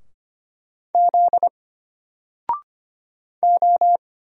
Morse code, heard as a keyed tone, about 25 wpm, 700 Hz. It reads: Z O
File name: Single-Letter-Number-25wpm